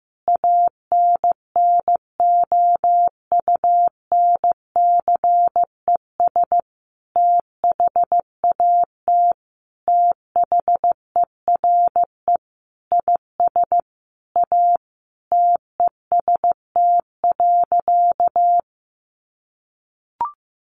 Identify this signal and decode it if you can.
Morse code, heard as a keyed tone, about 15 wpm, 700 Hz. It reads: ANNOUNCES THAT THERE IS A TEST.
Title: Additional Easy Phrases 15wpm